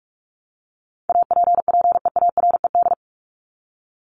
Morse code, heard as a keyed tone, about 44 wpm, 700 Hz. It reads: APPEARED
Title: ED Suffix - Top 100 Words with Base Words 45wpm